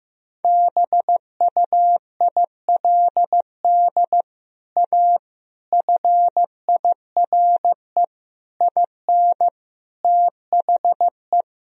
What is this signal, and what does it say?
Morse code, heard as a keyed tone, about 15 wpm, 700 Hz. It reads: BUILD A FIRE IN THE
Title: Sentences from Top 1000 Words - Review Part 4 - 15wpm